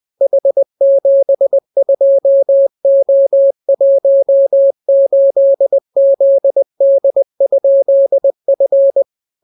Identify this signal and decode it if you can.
Morse code, heard as a keyed tone, about 20 wpm, 550 Hz. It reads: H72O18ZD?F